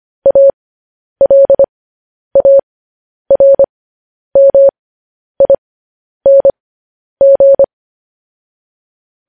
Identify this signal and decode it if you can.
Morse code, heard as a keyed tone, about 25 wpm, spaced as for 10 wpm, 550 Hz. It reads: ALARMING